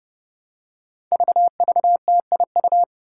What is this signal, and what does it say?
Morse code, heard as a keyed tone, about 30 wpm, 700 Hz. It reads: VVTIU
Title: Five Challenging Letter-Number Combinations 30wpm